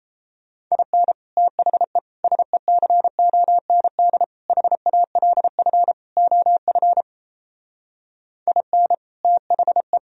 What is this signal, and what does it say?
Morse code, heard as a keyed tone, about 38 wpm, 700 Hz. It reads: IN THE SECOND HALF OF IN THE